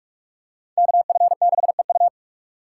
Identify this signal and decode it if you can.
Morse code, heard as a keyed tone, about 44 wpm, 700 Hz. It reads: KF6EU